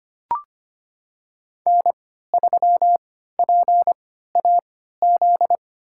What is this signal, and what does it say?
Morse code, heard as a keyed tone, about 25 wpm, spaced as for 14 wpm, 700 Hz. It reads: N3PAZ
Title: USA Call Signs - 3x Character Spacing - 25wpm